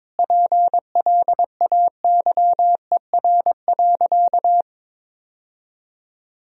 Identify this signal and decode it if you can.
Morse code, heard as a keyed tone, about 22 wpm, 700 Hz. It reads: PLAYER.
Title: Sentences from Top 1000 Words - Review Part 4 - 22wpm